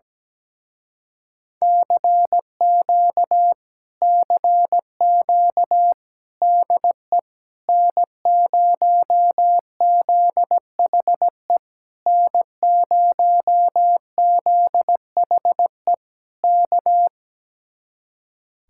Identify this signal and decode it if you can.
Morse code, heard as a keyed tone, about 17 wpm, 700 Hz. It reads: CQ CQ DE N0ZHE N0ZHE K
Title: Speed-Racing Common QSO Phrases - 17wpm